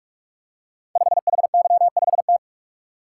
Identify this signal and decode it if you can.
Morse code, heard as a keyed tone, about 44 wpm, 700 Hz. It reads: 5HY5T
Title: Code Groups - 5 Letter:Numbers - 45wpm